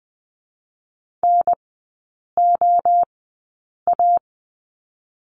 Morse code, heard as a keyed tone, about 21 wpm, 700 Hz. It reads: N O A
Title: 023 Lesson 5e - Letter o - Three Characters 20wpm